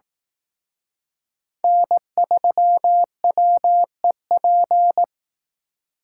Morse code, heard as a keyed tone, about 18 wpm, 700 Hz. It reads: N3WEP
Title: Speed Racing - USA Call Signs 15wpm